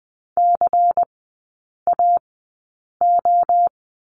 Morse code, heard as a keyed tone, about 19 wpm, spaced as for 8 wpm, 700 Hz. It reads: CAO
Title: 133 Lesson 22e - Question Mark - Three Characters 20wpm